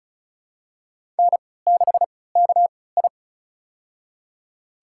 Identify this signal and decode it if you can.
Morse code, heard as a keyed tone, about 34 wpm, spaced as for 20 wpm, 700 Hz. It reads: N6KI
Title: USA Call Signs - 3x Character Spacing - 35wpm